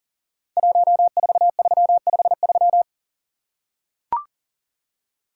Morse code, heard as a keyed tone, about 42 wpm, 700 Hz. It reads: 14353
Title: Code Groups - 5 Letter:Numbers - 40wpm